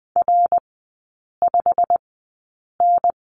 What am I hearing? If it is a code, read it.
Morse code, heard as a keyed tone, about 20 wpm, spaced as for 8 wpm, 700 Hz. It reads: R5N
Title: 152 Lesson 25e - Letter p - Three Characters 20wpm